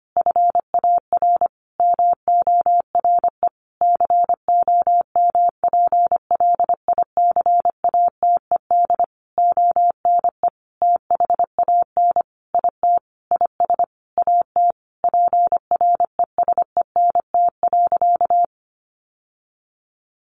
Morse code, heard as a keyed tone, about 25 wpm, 700 Hz. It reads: FAR MORE COMPLICATED ONE THAN IT IS AT PRESENT.
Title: Sentences from Top 1200 Words 25wpm